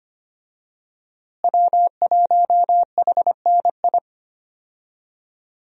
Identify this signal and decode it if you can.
Morse code, heard as a keyed tone, about 25 wpm, 700 Hz. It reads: W1HNI
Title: USA Call Signs - 5 Characters - Repeated 2x - 25wpm